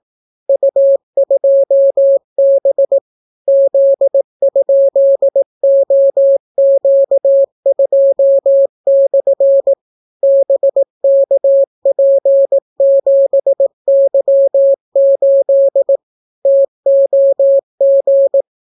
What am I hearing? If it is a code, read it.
Morse code, heard as a keyed tone, about 17 wpm, 550 Hz. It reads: U2B Z?OQ2/ BKP7Y8 TOG